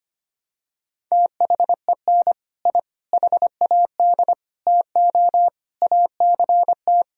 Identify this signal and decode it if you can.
Morse code, heard as a keyed tone, about 25 wpm, 700 Hz. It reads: THEN I HAD TO ACT
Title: Sentences from Top 1000 Words - Review Part 1 - 25wpm